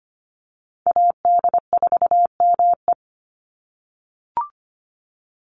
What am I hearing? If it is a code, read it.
Morse code, heard as a keyed tone, about 25 wpm, 700 Hz. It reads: AD4ME
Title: USA Call Signs 25wpm